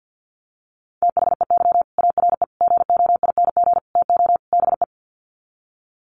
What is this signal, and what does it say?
Morse code, heard as a keyed tone, about 50 wpm, 700 Hz. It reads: THEY ARE GOING TO BE
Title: Sets of 5 Words Encore2 50wpm